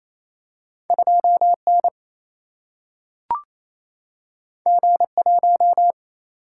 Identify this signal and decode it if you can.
Morse code, heard as a keyed tone, about 28 wpm, 700 Hz. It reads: 2N G1
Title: Single-Letter Single-Number 28wpm